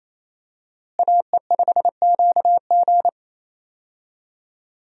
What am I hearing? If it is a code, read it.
Morse code, heard as a keyed tone, about 28 wpm, 700 Hz. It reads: AE5QG
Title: USA Call Signs - 5 Characters - Repeated 3x - 28wpm